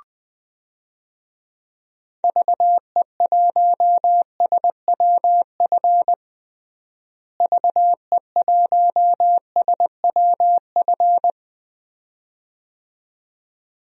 Morse code, heard as a keyed tone, about 20 wpm, 700 Hz. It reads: VE1SWF VE1SWF
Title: DXCC-100 Call Signs - 6 Characters - Repeated 2x - 20wpm